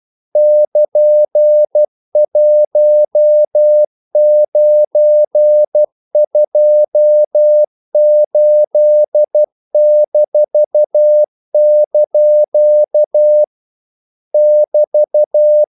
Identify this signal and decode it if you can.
Morse code, heard as a keyed tone, about 12 wpm, 600 Hz. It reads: <KN>1928-) <BT>
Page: English quotations in CW (telegraphy - morse)